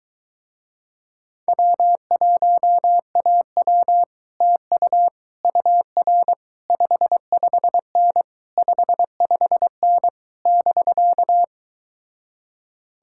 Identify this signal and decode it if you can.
Morse code, heard as a keyed tone, about 23 wpm, 700 Hz. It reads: W1AW TU UR 55N 55N <BK>